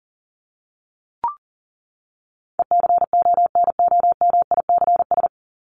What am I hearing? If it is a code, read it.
Morse code, heard as a keyed tone, about 40 wpm, 700 Hz. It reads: ECONOMICS
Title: Top 501-750 Words 40wpm